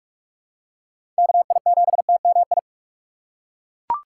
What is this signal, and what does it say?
Morse code, heard as a keyed tone, about 41 wpm, 700 Hz. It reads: KI7TMI
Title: USA Call Signs - 4x Character Spacing - 45wpm